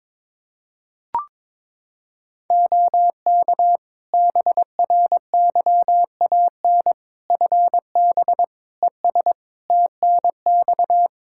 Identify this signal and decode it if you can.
Morse code, heard as a keyed tone, about 22 wpm, 700 Hz. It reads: OK BRYAN FB ES TNX
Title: Practice QSO - Exchange 2 - Info - 22wpm